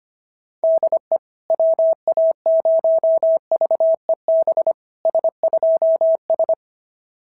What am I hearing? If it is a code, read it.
Morse code, heard as a keyed tone, about 25 wpm, 690 Hz. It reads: DE WA0VEB S2S
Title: SOTA QSO Practice 25wpm